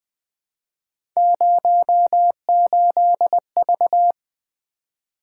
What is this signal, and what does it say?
Morse code, heard as a keyed tone, about 20 wpm, 700 Hz. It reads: 08V
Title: Three Letter-Number Combinations - 20wpm